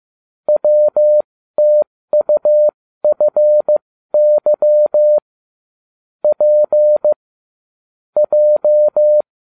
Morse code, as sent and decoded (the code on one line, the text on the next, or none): .-- - ..- ..-. -.-- / .--. / .---
WTUFY P J